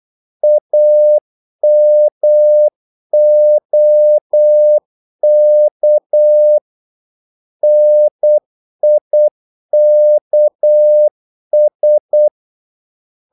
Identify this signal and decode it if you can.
Morse code, heard as a keyed tone, about 8 wpm, 600 Hz. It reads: AMOK NIKS
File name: Korte ord DK 8wpm